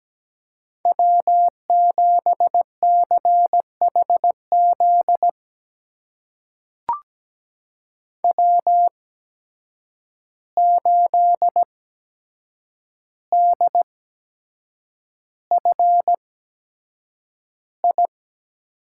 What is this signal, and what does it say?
Morse code, heard as a keyed tone, about 17 wpm, spaced as for 7 wpm, 700 Hz. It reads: W7CHZ W8DFI
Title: USA Call Signs - 5 Characters - 8x Character Spacing - 17wpm